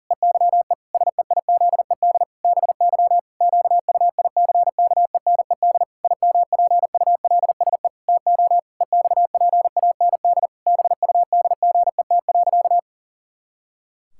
Morse code, heard as a keyed tone, about 40 wpm, 700 Hz. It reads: EYE SEIZED BY QUICKENED IMPULSE TO EXPAND BUDGET.
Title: Pangram Challenge v2 - 40wpm